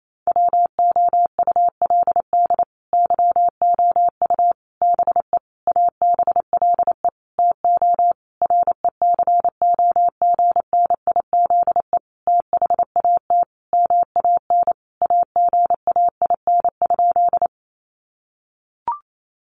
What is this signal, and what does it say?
Morse code, heard as a keyed tone, about 28 wpm, 700 Hz. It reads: WOULD YOU BE ABLE TO RECOGNIZE THAT MAN AGAIN?
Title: Sentences from Top 1400 Words 28wpm